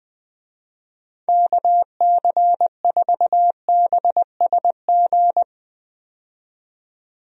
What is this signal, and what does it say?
Morse code, heard as a keyed tone, about 20 wpm, 700 Hz. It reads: KC4BSG
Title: USA Call Signs - 6 Characters - 20wpm